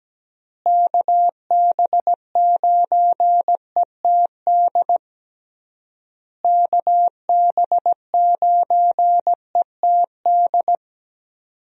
Morse code, heard as a keyed tone, about 17 wpm, 700 Hz. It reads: KB9ETD KB9ETD
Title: USA Call Signs - 6 Characters - Repeated 3x - 17wpm